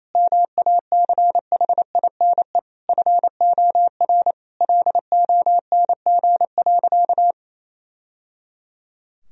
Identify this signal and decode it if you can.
Morse code, heard as a keyed tone, about 28 wpm, 700 Hz. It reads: MACHINE FOR LONG.
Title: Sentences from Top 1000 Words - Review Part 2 - 28wpm